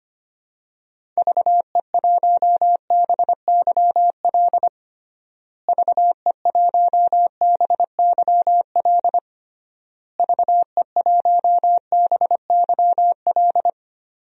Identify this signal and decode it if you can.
Morse code, heard as a keyed tone, about 25 wpm, 700 Hz. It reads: VE1BYL VE1BYL VE1BYL
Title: DXCC-100 Call Signs - 6 Characters - Repeated 3x - 25wpm